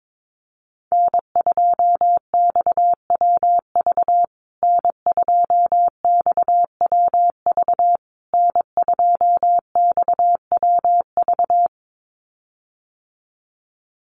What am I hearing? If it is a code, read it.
Morse code, heard as a keyed tone, about 22 wpm, 700 Hz. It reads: N2XWV N2XWV N2XWV
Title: USA Call Signs Repeated 3x 22wpm